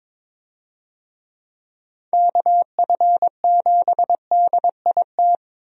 Morse code, heard as a keyed tone, about 22 wpm, 700 Hz. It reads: KF7DIT
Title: USA Call Signs - 6 Characters - Repeated 3x - 22wpm